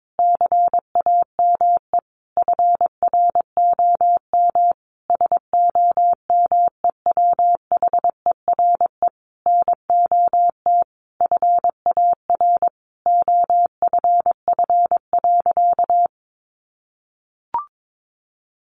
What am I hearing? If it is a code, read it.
Morse code, heard as a keyed tone, about 22 wpm, 700 Hz. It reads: CAME FROM SOMEWHERE NOT FAR OFF.
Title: Sentences from Top 1400 Words 22wpm